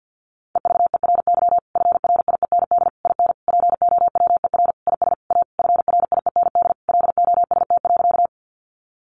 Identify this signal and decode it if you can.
Morse code, heard as a keyed tone, about 50 wpm, 700 Hz. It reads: EVERY FRIEND IN POWER IS A FRIEND LOST.
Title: Sentences from Top 1400 Words 50wpm